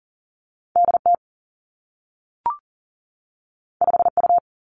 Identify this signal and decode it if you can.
Morse code, heard as a keyed tone, about 42 wpm, 700 Hz. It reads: D T 5 U
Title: Two Challenging Letter-Number Combinations 40wpm